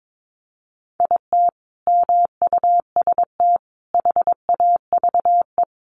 Morse code, heard as a keyed tone, about 22 wpm, 700 Hz. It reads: IT MUST HAVE
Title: Sentences from Top 800 Words 22wpm